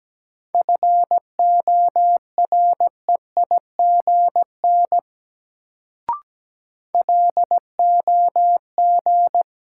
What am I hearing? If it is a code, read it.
Morse code, heard as a keyed tone, about 17 wpm, 700 Hz. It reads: FOREIGN LOG
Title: Sentences from Top 1400 Words - New Word Drill - 17wpm